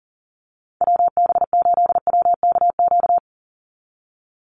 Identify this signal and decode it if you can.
Morse code, heard as a keyed tone, about 40 wpm, 700 Hz. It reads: WB8WKQ
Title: Call Signs - 40wpm